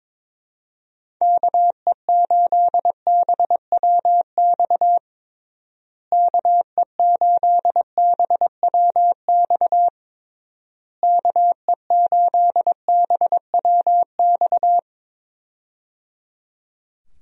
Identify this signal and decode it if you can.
Morse code, heard as a keyed tone, about 22 wpm, 700 Hz. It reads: KE8BWX KE8BWX KE8BWX
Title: USA Call Signs - 6 Characters - Repeated 3x - 22wpm